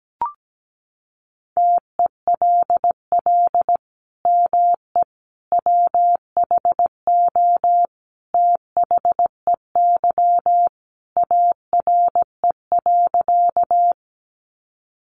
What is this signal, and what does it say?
Morse code, heard as a keyed tone, about 17 wpm, 700 Hz. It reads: TELL ME WHO THEY ARE.
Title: Sentences from Top 700 Words 17wpm